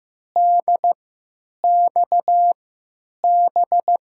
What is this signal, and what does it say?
Morse code, heard as a keyed tone, about 15 wpm, spaced as for 8 wpm, 700 Hz. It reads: DXB
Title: USA Call Signs - 3x Character Spacing - 15wpm